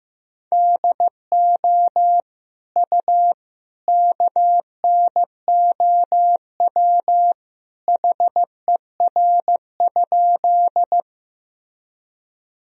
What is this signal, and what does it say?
Morse code, heard as a keyed tone, about 16 wpm, 700 Hz. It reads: DO U KNOW HER?
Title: Sentences from Top 100 Words and Abbreviations 15wpm